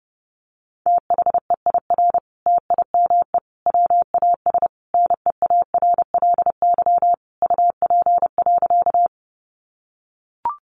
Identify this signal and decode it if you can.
Morse code, heard as a keyed tone, about 30 wpm, 700 Hz. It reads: THEIR TIME WAS NEARLY UP.
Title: Sentences from Top 1300 Words 30wpm